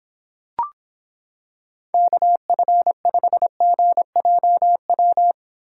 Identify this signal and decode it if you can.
Morse code, heard as a keyed tone, about 25 wpm, 700 Hz. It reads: KF5GJW